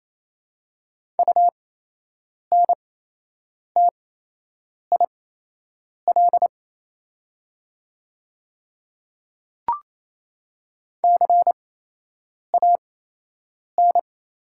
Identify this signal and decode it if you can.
Morse code, heard as a keyed tone, about 28 wpm, spaced as for 8 wpm, 700 Hz. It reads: UNTIL CAN
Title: 8x Character Spacing - Top 500 Words - 28wpm